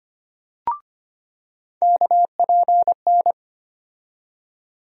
Morse code, heard as a keyed tone, about 25 wpm, 700 Hz. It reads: KPN
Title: USA Call Sign Postfixes - 3 Letters 25wpm